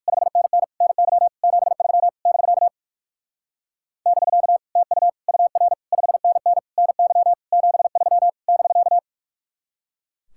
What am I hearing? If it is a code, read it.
Morse code, heard as a keyed tone, about 52 wpm, 700 Hz. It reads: 5NN NY 73 <BK> <BK> TU UR 5NN NY 73 <BK>
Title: Speed-Racing Common QSO Phrases Encore - 45wpm